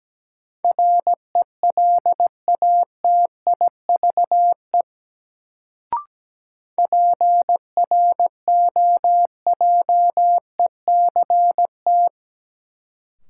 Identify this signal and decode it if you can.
Morse code, heard as a keyed tone, about 17 wpm, 700 Hz. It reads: RELATIVE PROJECT
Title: Sentences from Top 1400 Words - New Word Drill - 17wpm